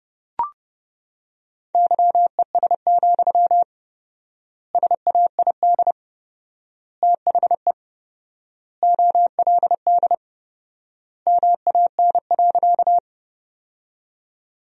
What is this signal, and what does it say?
Morse code, heard as a keyed tone, about 32 wpm, 700 Hz. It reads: YES, SAID THE OLD MAN.
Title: 4x Word-Spacing - Sentences from Top 500 Words - 30wpm